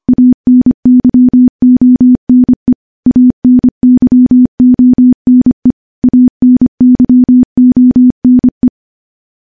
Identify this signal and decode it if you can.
Morse code, heard as a keyed tone, about 25 wpm, 250 Hz. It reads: ANYONE ANYONE ANYONE
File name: morse-training-25WPM